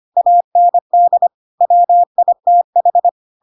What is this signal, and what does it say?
Morse code, heard as a keyed tone, about 25 wpm, 700 Hz. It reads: AND WITH
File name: Pinocchio / Pinocchio ch3 25 25 ik1hge